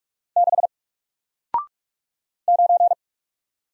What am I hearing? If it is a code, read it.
Morse code, heard as a keyed tone, about 44 wpm, 700 Hz. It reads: 6 9
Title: Single Letter-Number Alt 45wpm